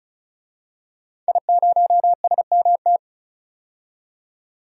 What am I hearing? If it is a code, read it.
Morse code, heard as a keyed tone, about 34 wpm, 700 Hz. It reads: I0SMT
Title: DXCC-100 Call Signs - 5 Characters - 35wpm